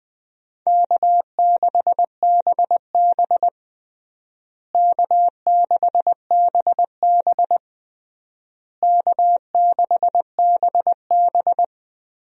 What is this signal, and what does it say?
Morse code, heard as a keyed tone, about 21 wpm, 700 Hz. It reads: K6BB K6BB K6BB
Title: USA Call Signs - 4 Characters - Repeated 3x - 20wpm